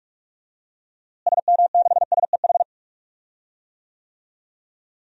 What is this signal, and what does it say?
Morse code, heard as a keyed tone, about 42 wpm, 700 Hz. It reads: SM6SEH
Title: DXCC-100 Call Signs - 6 Characters - Repeated 2x - 45wpm